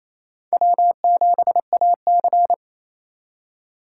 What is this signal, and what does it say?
Morse code, heard as a keyed tone, about 35 wpm, 700 Hz. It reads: W7AC
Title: Speed Racing - USA 1x2 Call Signs 28wpm